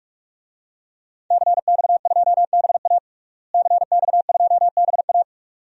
Morse code, heard as a keyed tone, about 42 wpm, 700 Hz. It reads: CX2BA CX2BA
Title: DXCC-100 Call Signs - 5 Characters - Repeated 2x - 45wpm